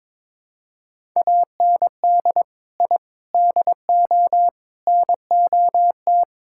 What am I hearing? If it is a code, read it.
Morse code, heard as a keyed tone, about 22 wpm, 700 Hz. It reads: AND I DO NOT
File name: practice / 22wpm / Build A Phrase - 5-Word Phrases - 22wpm